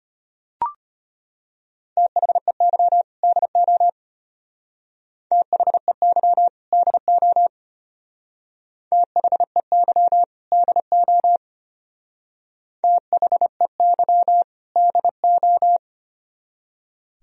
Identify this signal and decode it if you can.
Morse code, heard as a keyed tone, about 29 wpm, 700 Hz. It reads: THEY DO THEY DO THEY DO THEY DO
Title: Speed-Racing - Sets of 2 Words 25wpm